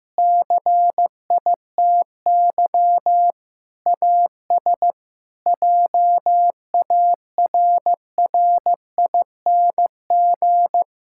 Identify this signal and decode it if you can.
Morse code, heard as a keyed tone, about 15 wpm, 700 Hz. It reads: CITY AS JARRING
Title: Pangram Challenge v2 - 15wpm